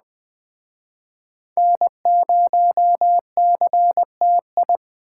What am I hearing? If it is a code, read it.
Morse code, heard as a keyed tone, about 21 wpm, 700 Hz. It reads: N0CTI